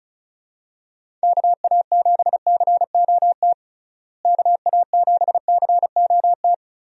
Morse code, heard as a keyed tone, about 36 wpm, 700 Hz. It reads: KA7COT KA7COT
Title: USA Call Signs - 6 Characters - Repeated 2x - 35wpm